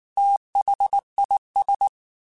Morse code, heard as a keyed tone, about 19 wpm, 800 Hz. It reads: THIS
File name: Technical Subjects / Technical Subjects-mp3s / XC @18